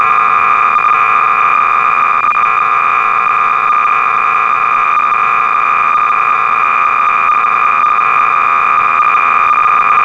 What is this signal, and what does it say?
Cassette data (Kansas City type).
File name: sorcerer / 300baud / Salvo-(1979)(J.T.Staley)-BASIC-Sorcerer-300baud